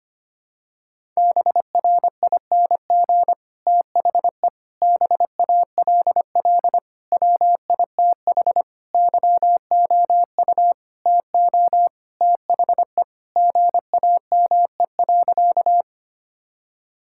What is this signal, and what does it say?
Morse code, heard as a keyed tone, about 25 wpm, 700 Hz. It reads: BRING THE BALL WITH YOU TO THE GAME.
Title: Sentences from Top 1000 Words - Review Part 6 - 25wpm